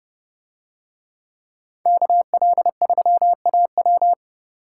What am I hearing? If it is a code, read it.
Morse code, heard as a keyed tone, about 30 wpm, 700 Hz. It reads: KL3AW